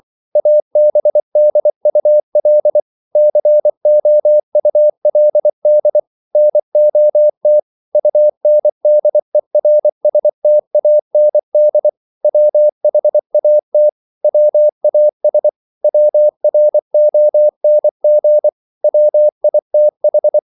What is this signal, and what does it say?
Morse code, heard as a keyed tone, about 24 wpm, 600 Hz. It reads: ABDUL COULD NOT UNDERSTAND WHAT WAS WRONG WITH